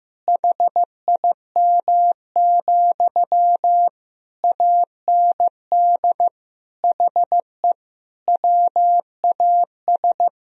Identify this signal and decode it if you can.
Morse code, heard as a keyed tone, about 15 wpm, 700 Hz. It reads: HIM, AND HE WAS
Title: Sentences from Top 1000 Words - Review Part 4 - 15wpm